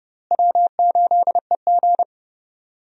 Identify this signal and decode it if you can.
Morse code, heard as a keyed tone, about 30 wpm, 700 Hz. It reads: W8EG
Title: USA Call Signs - 4 Characters - Repeated 3x - 30wpm